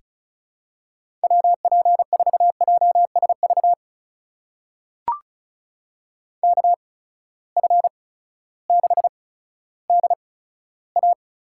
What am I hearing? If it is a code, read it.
Morse code, heard as a keyed tone, about 36 wpm, spaced as for 16 wpm, 700 Hz. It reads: WP4JSV KF6DA
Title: USA Call Signs - 6 Characters - 8x Character Spacing - 35wpm